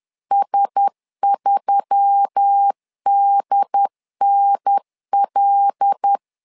Morse code, heard as a keyed tone, about 10 wpm, 800 Hz. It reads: S3DNL
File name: LEZIONI / 5 LBVC / 5 L 10wpm 2259